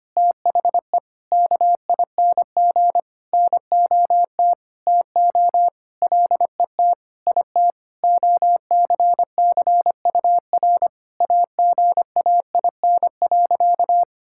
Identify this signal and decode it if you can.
Morse code, heard as a keyed tone, about 25 wpm, 700 Hz. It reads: THE KING NOT TO LET IT OCCUR AGAIN.
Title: Sentences from Top 1000 Words - Review Part 6 - 25wpm